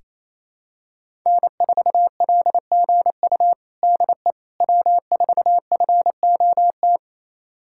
Morse code, heard as a keyed tone, about 28 wpm, 700 Hz. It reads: N4LGU DE W4FOT
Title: USA Callsign DE Callsign - 5 Characters - 28wpm